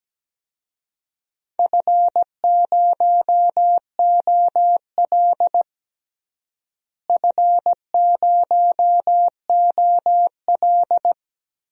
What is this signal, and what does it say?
Morse code, heard as a keyed tone, about 17 wpm, 700 Hz. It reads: F0OL F0OL
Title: DXCC-100 Call Signs - 4 Characters - Repeated 2x - 17wpm